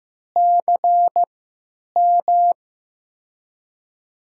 Morse code, heard as a keyed tone, about 15 wpm, spaced as for 9 wpm, 700 Hz. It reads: CM
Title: USA Call Signs - 3x Character Spacing - 15wpm